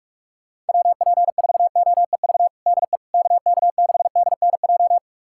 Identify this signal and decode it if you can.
Morse code, heard as a keyed tone, about 44 wpm, 700 Hz. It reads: WP4OEV DE KK6DNJ
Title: USA Callsign DE Callsign - 6 Characters - 45wpm